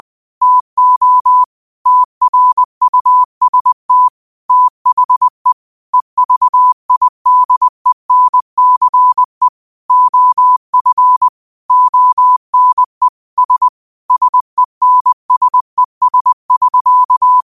Morse code, heard as a keyed tone, about 20 wpm, 1000 Hz. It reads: TO TRUST THE EVIDENCE OF ONE S SENSES<SK>